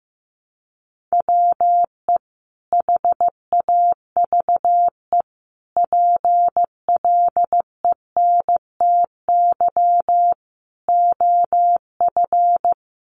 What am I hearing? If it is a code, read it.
Morse code, heard as a keyed tone, about 15 wpm, 700 Hz. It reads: WE HAVE PLENTY OF
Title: Sentences from Top 1300 Words 15wpm